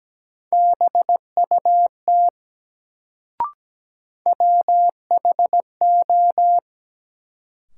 Morse code, heard as a keyed tone, about 17 wpm, 700 Hz. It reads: BUT WHO
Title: Sentences from Top 100 Words - New Word Drill - 17wpm